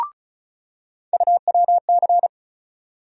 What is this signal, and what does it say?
Morse code, heard as a keyed tone, about 34 wpm, 700 Hz. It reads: UWC
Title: USA Call Sign Postfixes - 3 Letters 35wpm